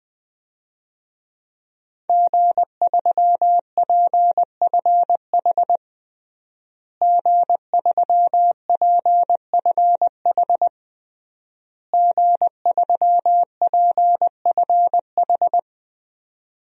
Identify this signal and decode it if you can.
Morse code, heard as a keyed tone, about 21 wpm, 700 Hz. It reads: G3PFH G3PFH G3PFH
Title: DXCC-100 Call Signs - 5 Characters - Repeated 3x - 20wpm